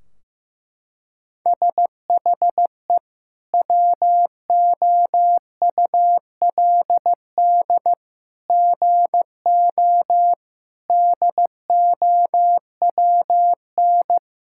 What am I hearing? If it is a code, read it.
Morse code, heard as a keyed tone, about 16 wpm, 700 Hz. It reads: SHE WOULD GO DOWN
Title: Sentences from Top 1600 Words 15wpm